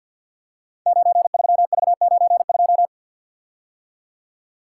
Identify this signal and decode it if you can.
Morse code, heard as a keyed tone, about 49 wpm, 700 Hz. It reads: 93V92
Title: Code Groups - 5 Letter:Numbers - 50wpm